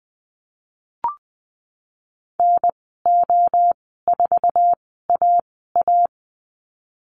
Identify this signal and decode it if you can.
Morse code, heard as a keyed tone, about 20 wpm, spaced as for 14 wpm, 700 Hz. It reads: NO4AA
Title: 076 Lesson 13g - Letter L - Call Signs 20wpm